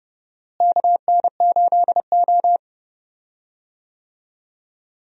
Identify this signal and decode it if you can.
Morse code, heard as a keyed tone, about 30 wpm, 700 Hz. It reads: KN8O
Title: USA Call Signs - 4 Characters - Repeated 2x - 30wpm